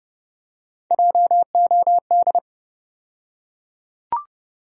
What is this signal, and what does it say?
Morse code, heard as a keyed tone, about 30 wpm, 700 Hz. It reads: JOD
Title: USA Call Sign Postfixes - 3 Letters 30wpm